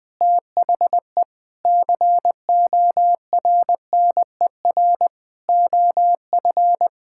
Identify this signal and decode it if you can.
Morse code, heard as a keyed tone, about 20 wpm, 700 Hz. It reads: THE CORNER OF
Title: Sentences with 7-Word Phrases - 20wpm